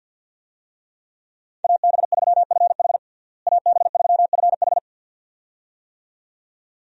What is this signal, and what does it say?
Morse code, heard as a keyed tone, about 52 wpm, 700 Hz. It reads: AB3FH AB3FH
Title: USA Call Signs - 5 Characters - Repeated 2x - 50wpm